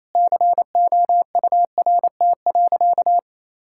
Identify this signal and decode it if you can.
Morse code, heard as a keyed tone, about 28 wpm, 700 Hz. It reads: COURT.
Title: Sentences from Top 1600 Words 28wpm